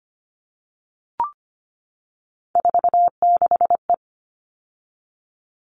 Morse code, heard as a keyed tone, about 25 wpm, 700 Hz. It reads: 46E
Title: All Cut Numbers - 3-Digit 25wpm